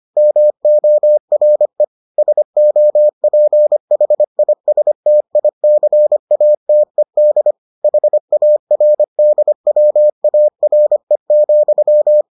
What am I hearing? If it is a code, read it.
Morse code, heard as a keyed tone, about 25 wpm, 600 Hz. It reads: MORE SOPHISTICATED HARDWARE,